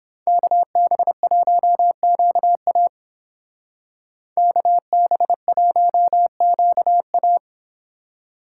Morse code, heard as a keyed tone, about 29 wpm, 700 Hz. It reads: KB1QA KB1QA